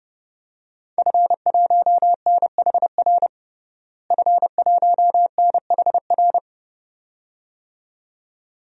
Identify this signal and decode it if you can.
Morse code, heard as a keyed tone, about 30 wpm, 700 Hz. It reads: F1NHR F1NHR
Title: DXCC-100 Call Signs - 5 Characters - Repeated 2x - 30wpm